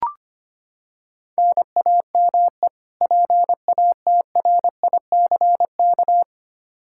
Morse code, heard as a keyed tone, about 25 wpm, 700 Hz. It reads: NAME PATRICK
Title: K1USN SST QSO Practice 25wpm